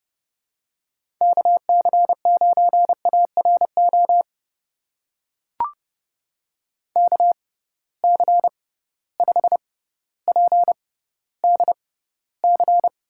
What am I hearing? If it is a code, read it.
Morse code, heard as a keyed tone, about 30 wpm, spaced as for 15 wpm, 700 Hz. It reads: KC9ARO KC5PDC
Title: USA Call Signs - 6 Characters - 6x Character Spacing - 30wpm